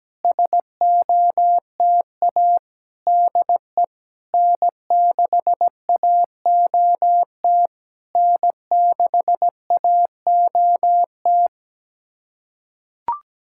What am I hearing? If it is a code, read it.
Morse code, heard as a keyed tone, about 17 wpm, 700 Hz. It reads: SOTA DE N6AOT N6AOT
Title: SOTA QSO Practice 17wpm